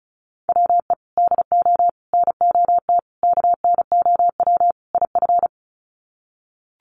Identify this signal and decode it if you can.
Morse code, heard as a keyed tone, about 34 wpm, 700 Hz. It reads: WE DO NOT KNOW IF
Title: Sets of 5 Words Encore2 35wpm